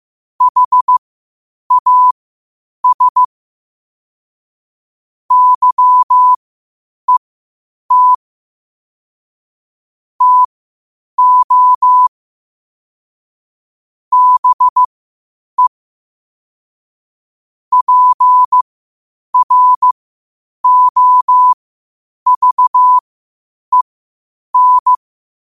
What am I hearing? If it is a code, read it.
Morse code, heard as a keyed tone, about 15 wpm, spaced as for 8 wpm, 1000 Hz. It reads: HAS YET TO BE PROVEN